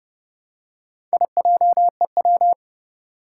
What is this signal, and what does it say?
Morse code, heard as a keyed tone, about 31 wpm, 700 Hz. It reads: IJEW
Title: Four Challenging Letter-Number Combinations 30wpm